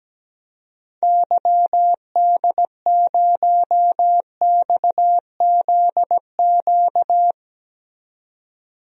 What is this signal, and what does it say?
Morse code, heard as a keyed tone, about 17 wpm, 700 Hz. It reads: YD0XZQ
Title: DXCC-100 Call Signs - 6 Characters - 17wpm